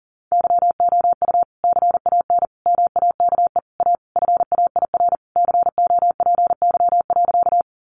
Morse code, heard as a keyed tone, about 37 wpm, 700 Hz. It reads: YOU CAN MAKE A FAIR COPY.
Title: Sentences from Top 1400 Words 40wpm